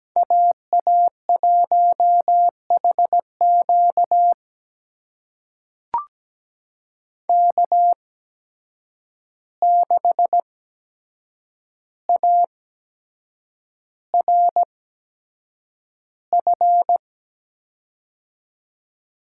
Morse code, heard as a keyed tone, about 17 wpm, spaced as for 7 wpm, 700 Hz. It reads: AA1HQ K6ARF
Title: USA Call Signs - 5 Characters - 8x Character Spacing - 17wpm